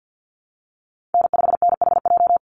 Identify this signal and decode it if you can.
Morse code, heard as a keyed tone, about 48 wpm, 700 Hz. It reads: N5NHJ
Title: Call Signs Repeated 2x - 50wpm